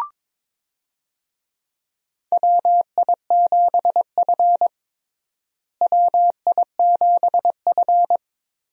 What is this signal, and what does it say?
Morse code, heard as a keyed tone, about 22 wpm, 700 Hz. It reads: WI7F WI7F
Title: USA Call Signs - 4 Characters - Repeated 3x - 22wpm